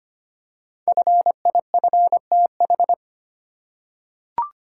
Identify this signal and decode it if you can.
Morse code, heard as a keyed tone, about 25 wpm, 700 Hz. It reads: FIFTH
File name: 5 Letter Words - First 3 Letters Given - 25wpm